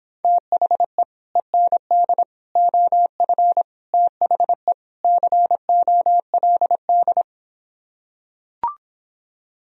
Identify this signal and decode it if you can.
Morse code, heard as a keyed tone, about 26 wpm, 700 Hz. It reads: THE END OF THE COLD